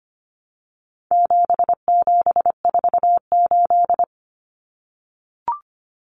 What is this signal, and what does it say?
Morse code, heard as a keyed tone, about 25 wpm, 700 Hz. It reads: 7748
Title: Common Cut Numbers - 4-Digit 25wpm